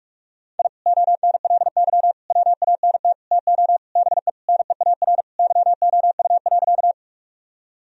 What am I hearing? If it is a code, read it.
Morse code, heard as a keyed tone, about 44 wpm, 700 Hz. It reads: I ONLY WANT TO BE NEAR YOU.
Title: Sentences from Top 1000 Words - Review Part 1 - 45wpm